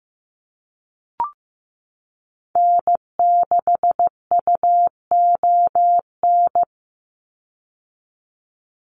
Morse code, heard as a keyed tone, about 15 wpm, 700 Hz. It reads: N6UON
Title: USA Call Signs 15wpm